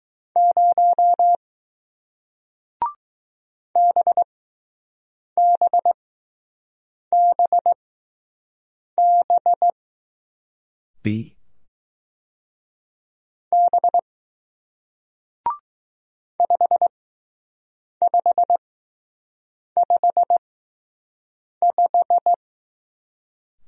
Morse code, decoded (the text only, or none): 0 BBBB B 5555